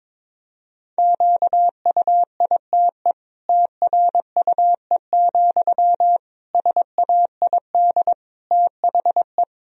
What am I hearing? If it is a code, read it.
Morse code, heard as a keyed tone, about 22 wpm, 700 Hz. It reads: QUITE TRUE, SAID THE
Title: Sentences from Top 1000 Words - Review Part 3 - 22wpm